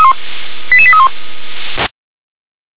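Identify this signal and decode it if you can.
Five-tone selective call (ZVEI), heard as quick runs of five tones, tones of about 70 ms. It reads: 88831
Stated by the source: ZVEI five-tone call 88912 received on SDR.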